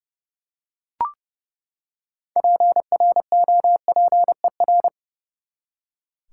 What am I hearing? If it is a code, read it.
Morse code, heard as a keyed tone, about 32 wpm, 700 Hz. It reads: PROPER
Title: Six Letter Words - 30wpm